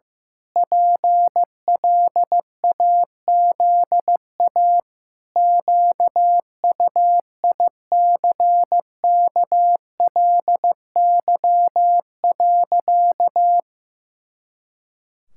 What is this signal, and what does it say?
Morse code, heard as a keyed tone, about 15 wpm, 700 Hz. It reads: PLAZA QUICKLY.
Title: Pangram Challenge v2 - 15wpm